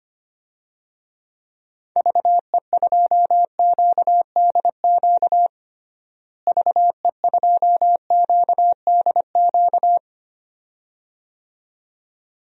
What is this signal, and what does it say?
Morse code, heard as a keyed tone, about 25 wpm, 700 Hz. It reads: VE2QDQ VE2QDQ
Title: DXCC-100 Call Signs - 6 Characters - Repeated 2x - 25wpm